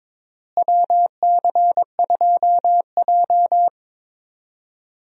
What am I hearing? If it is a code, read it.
Morse code, heard as a keyed tone, about 22 wpm, 700 Hz. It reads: WC2J
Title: USA Call Signs - 4 Characters - Repeated 3x - 22wpm